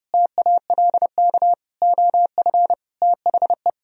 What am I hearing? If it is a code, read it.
Morse code, heard as a keyed tone, about 29 wpm, 700 Hz. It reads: TALK OF THE